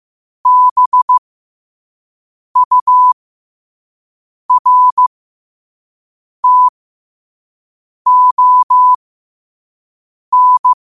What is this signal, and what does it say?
Morse code, heard as a keyed tone, about 15 wpm, spaced as for 5 wpm, 1000 Hz. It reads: BURTON
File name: qotdfiles / 5wpm